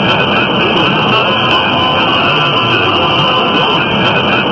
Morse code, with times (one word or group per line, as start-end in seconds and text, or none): none